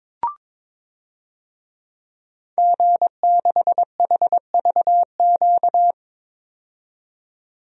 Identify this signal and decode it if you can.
Morse code, heard as a keyed tone, about 22 wpm, 700 Hz. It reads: G6HVQ
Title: DXCC-100 Call Signs - 5 Characters - 22wpm